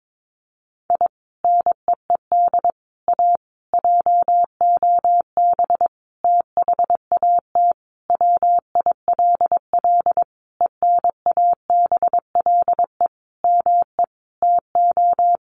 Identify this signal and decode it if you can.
Morse code, heard as a keyed tone, about 22 wpm, 700 Hz. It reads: I NEED A JOB THAT WILL ENABLE ME TO
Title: Sentences from Top 1500 Words 22wpm